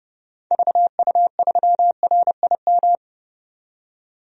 Morse code, heard as a keyed tone, about 30 wpm, 700 Hz. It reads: VU3RIM
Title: DXCC-100 Call Signs - 6 Characters - 30wpm